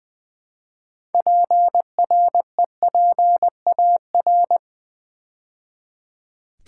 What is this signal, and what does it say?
Morse code, heard as a keyed tone, about 20 wpm, 700 Hz. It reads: PREPAR
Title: Build A Word - 2 Letters At A Time - 8-Letter Words - 20wpm